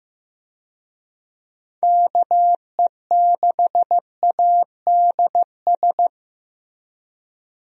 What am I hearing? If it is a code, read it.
Morse code, heard as a keyed tone, about 15 wpm, 700 Hz. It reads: KE6ADS
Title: USA Call Signs - 6 Characters - Repeated 2x - 15wpm